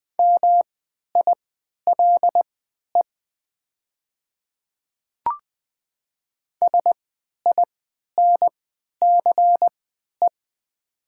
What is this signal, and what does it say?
Morse code, heard as a keyed tone, about 20 wpm, spaced as for 11 wpm, 700 Hz. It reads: MILE SINCE
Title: 3x Character Spacing - Top 300 Words - 20wpm